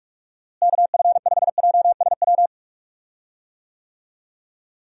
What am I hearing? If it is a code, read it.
Morse code, heard as a keyed tone, about 41 wpm, 700 Hz. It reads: KF5JSW
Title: USA Call Signs - 6 Characters - Repeated 3x - 45wpm